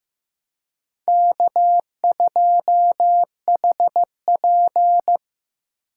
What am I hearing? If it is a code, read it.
Morse code, heard as a keyed tone, about 18 wpm, 700 Hz. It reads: K2HP
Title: Speed Racing - USA Call Signs 15wpm